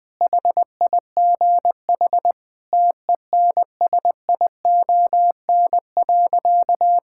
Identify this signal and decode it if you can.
Morse code, heard as a keyed tone, about 20 wpm, 700 Hz. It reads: HIGH TENSION.
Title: Sentences from Top 1600 Words 20wpm